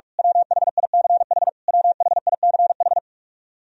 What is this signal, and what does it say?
Morse code, heard as a keyed tone, about 46 wpm, 700 Hz. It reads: WHICH WHICH
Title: Top 100 Words Repeated 45wpm